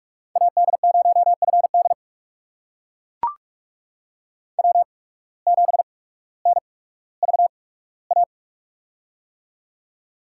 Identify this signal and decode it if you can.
Morse code, heard as a keyed tone, about 44 wpm, spaced as for 19 wpm, 700 Hz. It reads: AD0FD W7NVA
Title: USA Call Signs - 8x Character Spacing - 45wpm